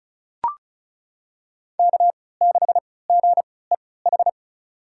Morse code, heard as a keyed tone, about 36 wpm, spaced as for 20 wpm, 700 Hz. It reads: K6GEH
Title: USA Call Signs - 3x Character Spacing - 35wpm